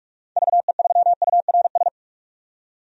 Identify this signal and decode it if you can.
Morse code, heard as a keyed tone, about 42 wpm, 700 Hz. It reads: VE3URS
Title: DXCC-100 Call Signs - 6 Characters - 45wpm